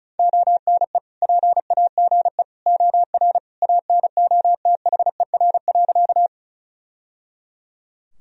Morse code, heard as a keyed tone, about 34 wpm, 690 Hz. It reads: ONE PAGE OR ANOTHER.
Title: Sentences from Top 1000 Words - Review Part 1 - 35wpm